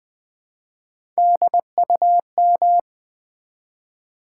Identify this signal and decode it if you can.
Morse code, heard as a keyed tone, about 20 wpm, 700 Hz. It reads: DUM
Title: Build A Word - 1 Letter At A Time - 3 to 5-Letter Words - 20wpm